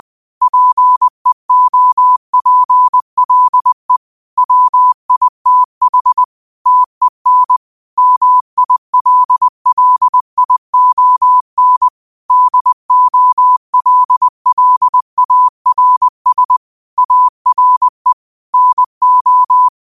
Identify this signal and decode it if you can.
Morse code, heard as a keyed tone, about 20 wpm, 1000 Hz. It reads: PEOPLE WITH TEN MILLION DOLLARS ARE NO